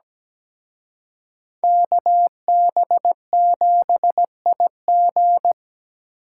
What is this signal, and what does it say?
Morse code, heard as a keyed tone, about 20 wpm, 700 Hz. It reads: KB7IG
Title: Speed Racing - USA 2x2 Call Signs 17wpm